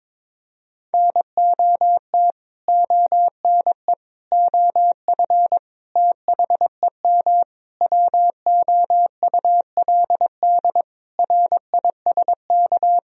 Morse code, heard as a keyed tone, about 22 wpm, 700 Hz. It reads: NOT ONE OF THEM WOULD RISK
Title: Sentences from Top 1000 Words - Review Part 4 - 22wpm